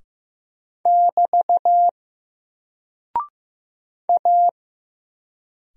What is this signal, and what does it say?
Morse code, heard as a keyed tone, about 15 wpm, 700 Hz. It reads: <BT> A
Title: Single Character (Letter:Number:Punctuation:Prosign) 15wpm